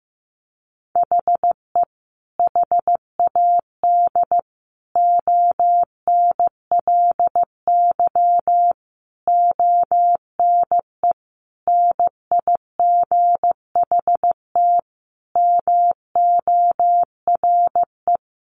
Morse code, heard as a keyed tone, about 15 wpm, 700 Hz. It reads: HE HAD ONLY ONE NIGHT MORE
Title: Sentences from Top 300 Words 15wpm